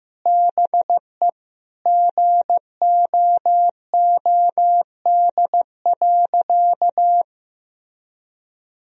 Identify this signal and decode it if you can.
Morse code, heard as a keyed tone, about 15 wpm, 700 Hz. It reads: BE GOOD.
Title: Sentences from Top 1000 Words - Review Part 3 - 15wpm